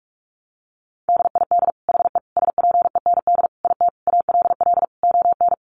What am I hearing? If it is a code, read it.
Morse code, heard as a keyed tone, about 44 wpm, 700 Hz. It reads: DID HE SPEND IT ALL ON
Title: Sentences from Top 800 Words 45wpm